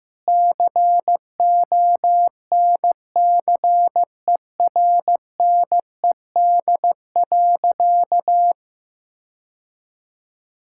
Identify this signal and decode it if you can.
Morse code, heard as a keyed tone, about 15 wpm, 700 Hz. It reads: CONCERNED.
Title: Sentences from Top 1000 Words - Review Part 6 - 15wpm